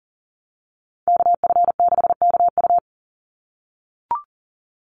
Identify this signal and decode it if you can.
Morse code, heard as a keyed tone, about 40 wpm, 700 Hz. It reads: KF6KU
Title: USA Call Signs 40wpm